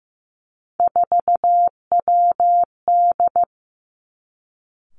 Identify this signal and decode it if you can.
Morse code, heard as a keyed tone, about 15 wpm, 700 Hz. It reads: 4WD
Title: Three Challenging Letter-Number Combinations 15wpm